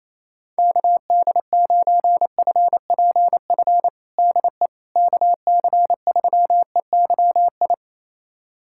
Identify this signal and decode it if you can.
Morse code, heard as a keyed tone, about 27 wpm, 700 Hz. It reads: KD9FPF DE KC3EYI
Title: USA Callsign DE Callsign - 6 Characters - 28wpm